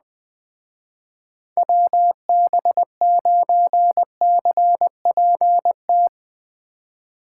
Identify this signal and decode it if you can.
Morse code, heard as a keyed tone, about 20 wpm, 700 Hz. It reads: WB9CPT